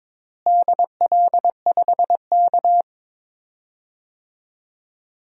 Morse code, heard as a keyed tone, about 22 wpm, 700 Hz. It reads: DL5K
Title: DXCC-100 Call Signs - 4 Characters - Repeated 2x - 22wpm